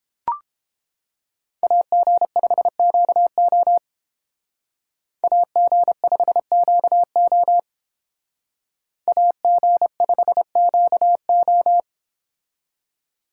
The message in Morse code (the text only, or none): AG5QO AG5QO AG5QO